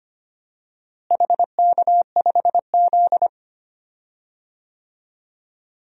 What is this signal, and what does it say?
Morse code, heard as a keyed tone, about 25 wpm, 700 Hz. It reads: HK5Z
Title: DXCC-100 Call Signs - 4 Characters - Repeated 2x - 25wpm